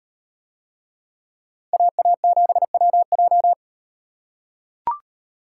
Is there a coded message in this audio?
Yes: Morse code from 1.7 to 3.5 s, heard as a keyed tone, about 38 wpm, 700 Hz. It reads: AA7WJ